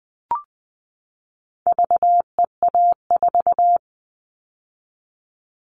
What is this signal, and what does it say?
Morse code, heard as a keyed tone, about 20 wpm, 700 Hz. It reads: VEA4
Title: All Cut Numbers - 4-Digit 20wpm